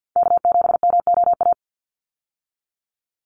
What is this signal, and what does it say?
Morse code, heard as a keyed tone, about 49 wpm, 700 Hz. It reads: K7MOA
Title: Call Signs Repeated 2x - 50wpm